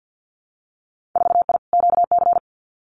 Morse code, heard as a keyed tone, about 50 wpm, 700 Hz. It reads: 4I QC
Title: Winter Field Day Exchanges Encore 50wpm